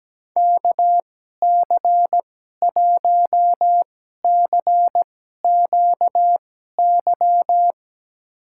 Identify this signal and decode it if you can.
Morse code, heard as a keyed tone, about 17 wpm, spaced as for 12 wpm, 700 Hz. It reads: KC1CQY
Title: USA Call Signs - 6 Characters - 2x Character Spacing - 17wpm